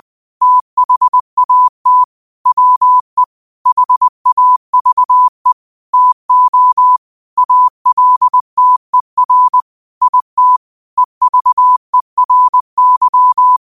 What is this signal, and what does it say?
Morse code, heard as a keyed tone, about 20 wpm, 1000 Hz. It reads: THAT WE HAVE TO ALTER IT EVERY